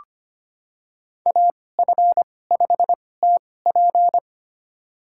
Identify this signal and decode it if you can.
Morse code, heard as a keyed tone, about 25 wpm, spaced as for 18 wpm, 700 Hz. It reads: AF5TP
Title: USA Call Signs - 5 Characters - 2x Character Spacing - 25wpm